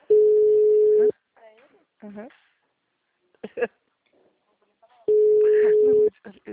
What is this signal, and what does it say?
Ringback tone (Europe, 425 Hz), with speech.
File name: out- -2011- - - .20